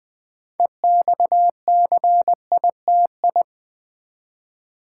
Morse code, heard as a keyed tone, about 20 wpm, 690 Hz. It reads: EXCITI
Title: Build A Word - 2 Letters At A Time - 8-Letter Words - 20wpm